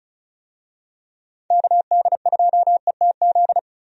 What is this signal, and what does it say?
Morse code, heard as a keyed tone, about 34 wpm, 700 Hz. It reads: KD2ETZ
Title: USA Call Signs - 6 Characters - Repeated 3x - 35wpm